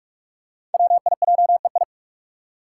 Morse code, heard as a keyed tone, about 44 wpm, 700 Hz. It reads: WIJEI
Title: Five Challenging Letter-Number Combinations 45wpm